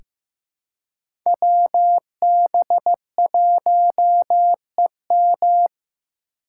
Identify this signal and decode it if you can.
Morse code, heard as a keyed tone, about 15 wpm, 700 Hz. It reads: WB1EM
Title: Code Groups - 5 Letter:Numbers - 15wpm